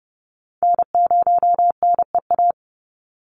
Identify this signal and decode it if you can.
Morse code, heard as a keyed tone, about 30 wpm, 700 Hz. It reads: N0NEA
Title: USA Call Signs 30wpm